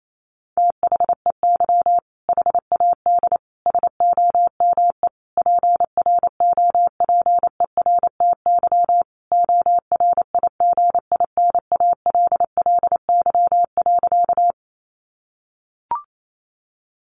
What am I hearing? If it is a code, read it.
Morse code, heard as a keyed tone, about 28 wpm, 700 Hz. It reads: THEY HAD SOME PROPERTY ORIGINALLY.
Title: Sentences from Top 1500 Words 28wpm